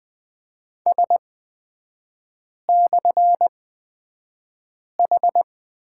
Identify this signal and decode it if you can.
Morse code, heard as a keyed tone, about 20 wpm, spaced as for 5 wpm, 700 Hz. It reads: S/H
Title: 243 Lesson 39e - Letter z - Three Characters 20wpm